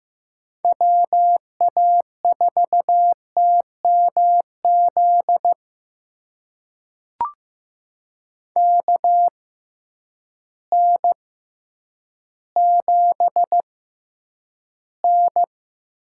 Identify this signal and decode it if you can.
Morse code, heard as a keyed tone, about 15 wpm, spaced as for 9 wpm, 700 Hz. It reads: WA4TMZ KN7N
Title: USA Call Signs - 6x Character Spacing - 15wpm